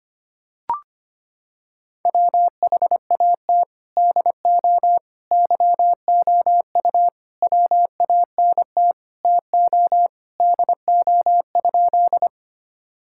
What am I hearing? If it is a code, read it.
Morse code, heard as a keyed tone, about 25 wpm, 700 Hz. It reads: WHAT DO YOU WANT TO DO?
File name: practice / 25wpm / Sentences with 3-Word Phrases - 25wpm